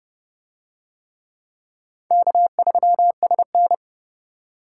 Morse code, heard as a keyed tone, about 27 wpm, 700 Hz. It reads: K3SN